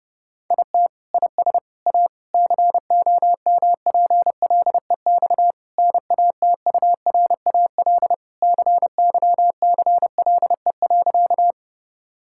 Morse code, heard as a keyed tone, about 31 wpm, 700 Hz. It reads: IT IS A COMPLEX NATURAL CYCLE.
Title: Sentences from Top 1000 Words - Review Part 6 - 30wpm